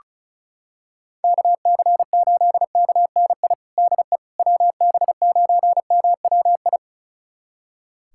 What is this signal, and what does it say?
Morse code, heard as a keyed tone, about 34 wpm, 700 Hz. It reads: KC8KNI DE WB9MWI
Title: USA Callsign DE Callsign 35wpm